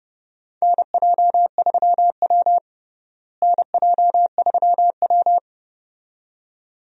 Morse code, heard as a keyed tone, about 31 wpm, 700 Hz. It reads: NJ3W NJ3W
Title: USA Call Signs - 4 Characters - Repeated 2x - 30wpm